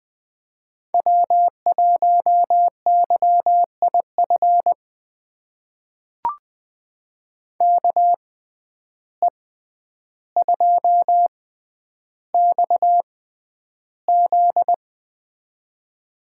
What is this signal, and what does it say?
Morse code, heard as a keyed tone, about 20 wpm, spaced as for 10 wpm, 700 Hz. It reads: W1YIF KE2XZ
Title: USA Call Signs - 6x Character Spacing - 20wpm